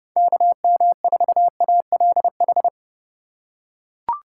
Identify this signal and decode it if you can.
Morse code, heard as a keyed tone, about 30 wpm, 700 Hz. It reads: KM4ALH
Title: USA Call Signs - 4x Character Spacing - 30wpm